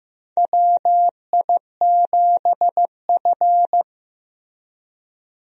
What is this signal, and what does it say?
Morse code, heard as a keyed tone, about 15 wpm, 700 Hz. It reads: WI7F
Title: USA Call Signs - 4 Characters - Repeated 3x - 15wpm